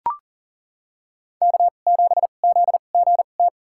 Morse code, heard as a keyed tone, about 40 wpm, spaced as for 29 wpm, 700 Hz. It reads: K7ZGT
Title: USA Call Signs - 2x Character Spacing - 40wpm